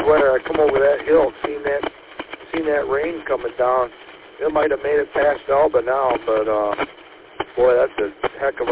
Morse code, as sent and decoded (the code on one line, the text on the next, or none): none